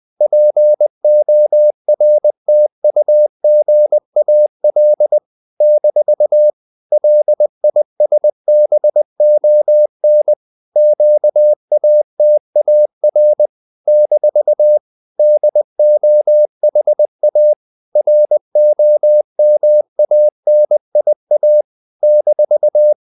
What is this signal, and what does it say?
Morse code, heard as a keyed tone, about 20 wpm, 600 Hz. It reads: PORTUGAL - LISBON QATAR - DOHA ROMANIA -
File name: CW / Practice / NationalCapitals20